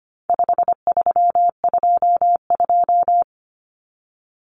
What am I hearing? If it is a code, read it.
Morse code, heard as a keyed tone, about 25 wpm, 700 Hz. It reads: 5322
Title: Common Cut Numbers - 4-Digit 25wpm